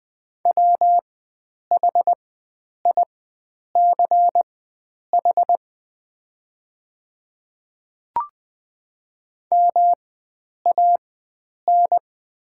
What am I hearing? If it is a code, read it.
Morse code, heard as a keyed tone, about 20 wpm, spaced as for 9 wpm, 700 Hz. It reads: WHICH MAN
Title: 4x Character Spacing - Top 500 Words - 20wpm